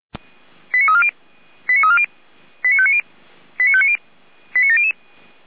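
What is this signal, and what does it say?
Five-tone selective call (ZVEI), heard as quick runs of five tones, tones of about 70 ms. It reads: 89340 89340 89590 89590 89700
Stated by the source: ZVEI five-tone selective call 89330 received on an SDR.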